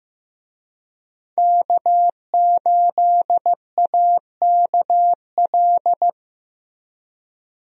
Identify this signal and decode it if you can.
Morse code, heard as a keyed tone, about 15 wpm, 700 Hz. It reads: K8AKL
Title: USA Call Signs - 5 Characters - Repeated 2x - 15wpm